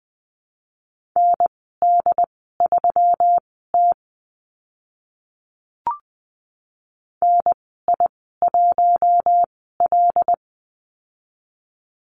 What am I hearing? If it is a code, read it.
Morse code, heard as a keyed tone, about 20 wpm, spaced as for 14 wpm, 700 Hz. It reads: ND3T NI1L
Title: 122 Lesson 20g - Number 3 - Call Signs 20wpm